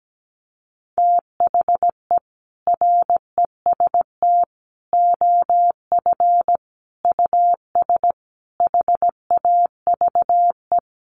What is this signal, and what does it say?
Morse code, heard as a keyed tone, about 17 wpm, 700 Hz. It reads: THE REST OF US HAVE
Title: Sentences from Top 700 Words 17wpm